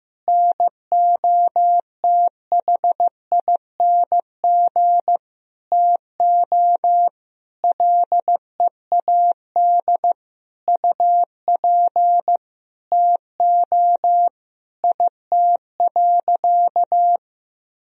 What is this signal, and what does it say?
Morse code, heard as a keyed tone, about 16 wpm, 700 Hz. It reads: NOTHING TO LEAD UP TO IT.
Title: Sentences from Top 1000 Words - Review Part 2 - 15wpm